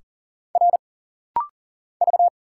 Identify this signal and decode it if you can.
Morse code, heard as a keyed tone, about 40 wpm, 700 Hz. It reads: R V
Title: Single Letters - Rapid-Fire 40wpm